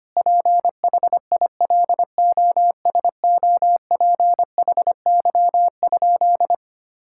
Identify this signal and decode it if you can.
Morse code, heard as a keyed tone, about 25 wpm, 700 Hz. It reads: PHILOSOPHY?
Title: Sentences from Top 1000 Words - Review Part 6 - 25wpm